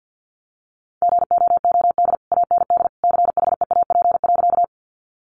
Morse code, heard as a keyed tone, about 50 wpm, 700 Hz. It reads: GOOD AND CHEAP.
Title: Sentences from Top 1200 Words 50wpm